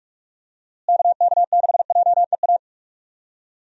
Morse code, heard as a keyed tone, about 41 wpm, 700 Hz. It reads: KK6JEA
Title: USA Call Signs - 6 Characters - Repeated 2x - 45wpm